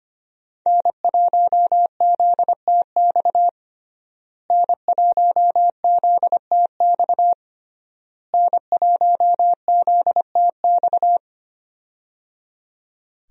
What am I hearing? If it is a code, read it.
Morse code, heard as a keyed tone, about 25 wpm, 700 Hz. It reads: N1ZTX N1ZTX N1ZTX
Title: USA Call Signs - 5 Characters - Repeated 3x - 25wpm